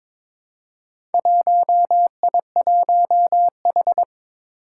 Morse code, heard as a keyed tone, about 23 wpm, 700 Hz. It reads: 1I1H
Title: Four Challenging Letter-Number Combinations 22wpm